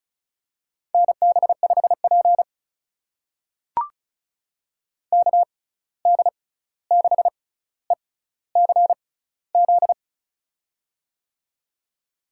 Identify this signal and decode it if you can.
Morse code, heard as a keyed tone, about 36 wpm, spaced as for 16 wpm, 700 Hz. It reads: NB5P KD6ECZ
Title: USA Call Signs - 6x Character Spacing - 35wpm